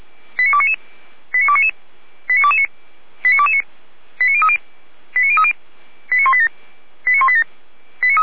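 Five-tone selective call (ZVEI), heard as quick runs of five tones, tones of about 70 ms. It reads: 89299 89299 89229 89229 89030 89030 89187 89187
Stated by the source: Five-tone ZVEI selective call received by SDR.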